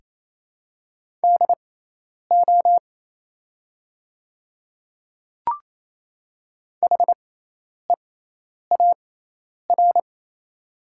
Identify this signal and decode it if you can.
Morse code, heard as a keyed tone, about 27 wpm, spaced as for 10 wpm, 700 Hz. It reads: DO HEAR
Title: 6x Character Spacing - Top 300 Words - 28wpm